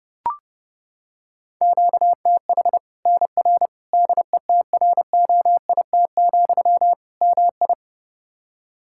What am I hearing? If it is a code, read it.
Morse code, heard as a keyed tone, about 32 wpm, 700 Hz. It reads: QTH NR DETROIT, MI
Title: K1USN SST QSO Practice 30wpm